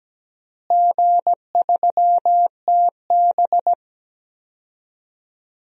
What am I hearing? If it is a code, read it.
Morse code, heard as a keyed tone, about 17 wpm, 700 Hz. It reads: G3TB
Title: DXCC-100 Call Signs - 4 Characters - Repeated 2x - 17wpm